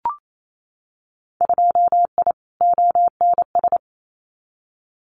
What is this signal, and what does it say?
Morse code, heard as a keyed tone, about 28 wpm, 700 Hz. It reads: 2I ONS
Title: Winter Field Day Exchanges Encore 28wpm